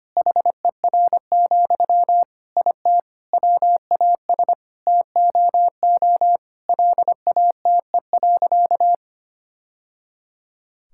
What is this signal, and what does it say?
Morse code, heard as a keyed tone, about 25 wpm, 700 Hz. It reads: HER, IT WAS TOO LATE.
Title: Sentences from Top 1000 Words - Review Part 6 - 25wpm